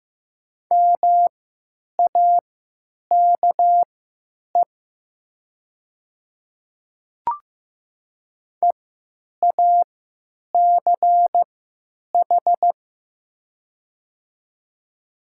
Morse code, heard as a keyed tone, about 15 wpm, spaced as for 9 wpm, 700 Hz. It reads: MAKE EACH
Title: 3x Character Spacing - Top 100 Words - 15wpm